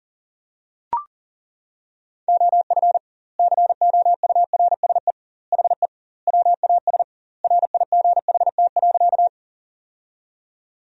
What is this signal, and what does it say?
Morse code, heard as a keyed tone, about 40 wpm, 700 Hz. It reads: OF COURSE HE WAS RIGHT.
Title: 2x Word-Spacing - Sentences from Top 500 Words - 40wpm